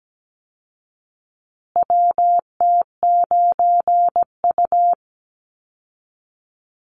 Morse code, heard as a keyed tone, about 17 wpm, 700 Hz. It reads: WT9U
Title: Call Signs Repeated 3x - 17wpm